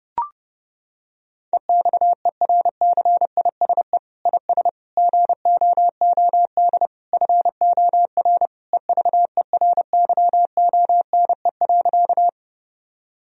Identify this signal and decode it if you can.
Morse code, heard as a keyed tone, about 30 wpm, 700 Hz. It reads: EXERCISE IS GOOD FOR EVERYONE.
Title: Sentences from Top 1000 Words - Review Part 4 - 30wpm